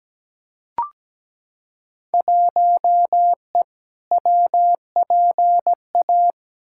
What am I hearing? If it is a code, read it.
Morse code, heard as a keyed tone, about 17 wpm, 700 Hz. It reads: 1E WPA
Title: Field Day Exchanges - 17wpm